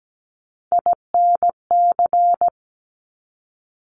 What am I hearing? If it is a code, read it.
Morse code, heard as a keyed tone, about 17 wpm, 700 Hz. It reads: INC
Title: Top Three Letter Combinations in Words 17wpm